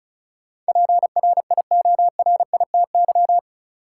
Morse code, heard as a keyed tone, about 36 wpm, 700 Hz. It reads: PRIORITY
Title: Sentences from Top 1600 Words - New Word Drill - 35wpm